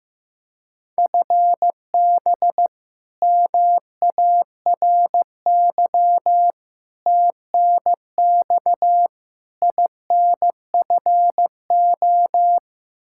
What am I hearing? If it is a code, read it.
Morse code, heard as a keyed tone, about 15 wpm, 700 Hz. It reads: FB MARY TNX INFO
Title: Practice QSO - Exchange 3 - More Info - 15wpm